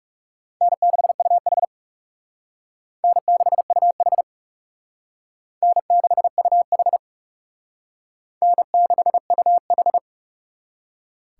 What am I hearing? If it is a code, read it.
Morse code, heard as a keyed tone, about 37 wpm, 700 Hz. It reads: N6UH N6UH N6UH N6UH
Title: Speed Racing - USA 1x2 Call Signs 30wpm